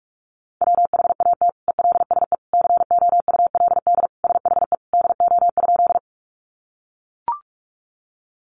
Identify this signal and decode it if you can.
Morse code, heard as a keyed tone, about 44 wpm, 700 Hz. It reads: WHAT ELSE COULD SHE DO?
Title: Sentences from Top 800 Words 45wpm